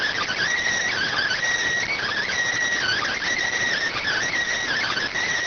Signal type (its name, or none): Emergency Alert System (SAME) tones